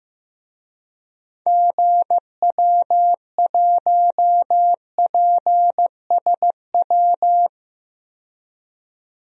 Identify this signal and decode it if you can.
Morse code, heard as a keyed tone, about 15 wpm, 700 Hz. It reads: GW1PSW
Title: DXCC-100 Call Signs - 6 Characters - 15wpm